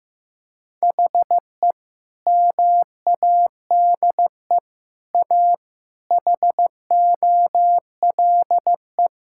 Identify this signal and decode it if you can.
Morse code, heard as a keyed tone, about 15 wpm, 700 Hz. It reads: HE MADE A HOLE
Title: Sentences from Top 1600 Words 15wpm